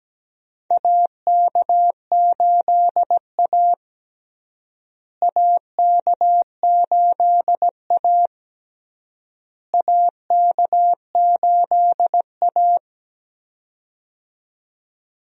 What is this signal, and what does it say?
Morse code, heard as a keyed tone, about 17 wpm, 700 Hz. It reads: AK8A AK8A AK8A
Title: USA Call Signs - 4 Characters - Repeated 3x - 17wpm